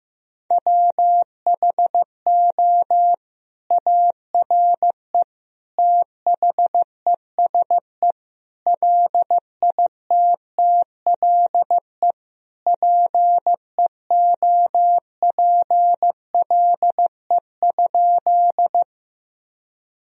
Morse code, heard as a keyed tone, about 15 wpm, 700 Hz. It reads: WHO ARE THESE LITTLE PEOPLE?
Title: Sentences from Top 1000 Words - Review Part 1 - 15wpm